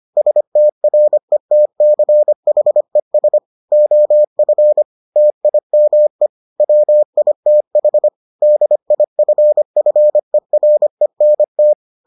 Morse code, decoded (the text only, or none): STRETCHES OF TIME WITH DIFFERENT